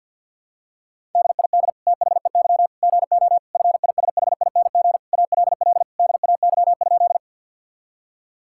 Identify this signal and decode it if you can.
Morse code, heard as a keyed tone, about 50 wpm, 700 Hz. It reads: DID THEY GO FISHING ALL DAY?
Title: Sentences from Top 1000 Words - Review Part 5 - 50wpm